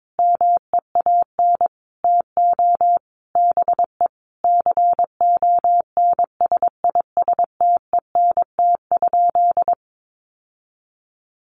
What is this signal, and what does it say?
Morse code, heard as a keyed tone, about 22 wpm, 700 Hz. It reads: MEAN TO BE CONSISTENT?
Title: Sentences from Top 1500 Words 22wpm